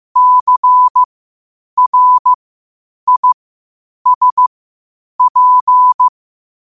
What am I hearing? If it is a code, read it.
Morse code, heard as a keyed tone, about 15 wpm, spaced as for 8 wpm, 1000 Hz. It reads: CRISP
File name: qotdfiles / 8wpm